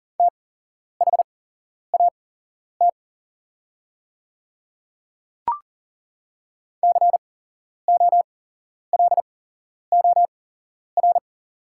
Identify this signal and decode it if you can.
Morse code, heard as a keyed tone, about 40 wpm, spaced as for 11 wpm, 700 Hz. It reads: THAT COLOR
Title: 8x Character Spacing - Top 500 Words - 40wpm